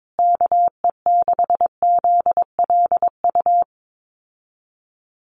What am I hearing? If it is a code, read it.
Morse code, heard as a keyed tone, about 22 wpm, 700 Hz. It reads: KE6ZLU
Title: USA Call Signs Repeated 2x 22wpm